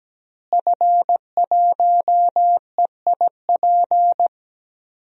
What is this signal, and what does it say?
Morse code, heard as a keyed tone, about 17 wpm, 700 Hz. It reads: F1EIP
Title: DXCC-100 Call Signs - 5 Characters - Repeated 3x - 17wpm